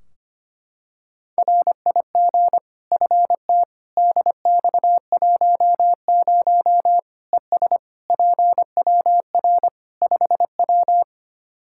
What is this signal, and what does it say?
Morse code, heard as a keyed tone, about 25 wpm, 700 Hz. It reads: RIG FT DX10 ES PWR 5W
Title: Practice QSO - Exchange 2 - Info - 25wpm